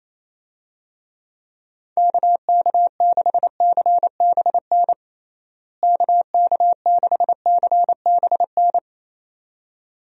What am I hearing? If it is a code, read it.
Morse code, heard as a keyed tone, about 27 wpm, 700 Hz. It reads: KK6CBN KK6CBN
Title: USA Call Signs - 6 Characters - Repeated 2x - 28wpm